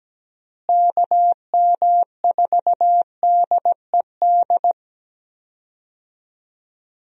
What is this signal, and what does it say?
Morse code, heard as a keyed tone, about 17 wpm, 700 Hz. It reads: KM4DED
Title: USA Call Signs - 6 Characters - Repeated 3x - 17wpm